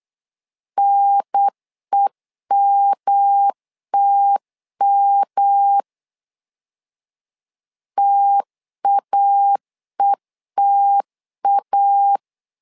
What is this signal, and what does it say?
Morse code, heard as a keyed tone, about 8 wpm, 800 Hz. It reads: NEMTM TAETA
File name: LEZIONI / 3 MWRO / 3 M 8wpm 2300